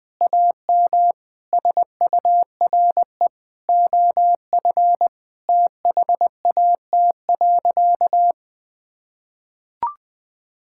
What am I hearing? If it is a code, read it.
Morse code, heard as a keyed tone, about 20 wpm, 700 Hz. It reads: AM SURE OF THAT.
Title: Sentences with 2-Word Phrases - 20wpm